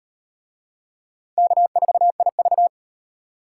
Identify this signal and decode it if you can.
Morse code, heard as a keyed tone, about 38 wpm, 700 Hz. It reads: K4IV